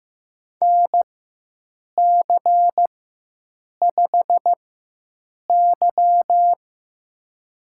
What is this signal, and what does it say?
Morse code, heard as a keyed tone, about 15 wpm, spaced as for 7 wpm, 700 Hz. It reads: NC5Y
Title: USA Call Signs - 4 Characters - 4x Character Spacing - 15wpm